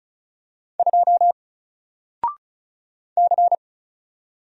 Morse code, heard as a keyed tone, about 35 wpm, 700 Hz. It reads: 2 C
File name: Single-Letter-Number-35wpm